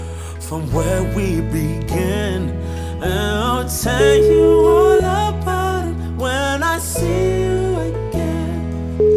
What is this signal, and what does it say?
Ringback tone (Europe, 425 Hz), with noise.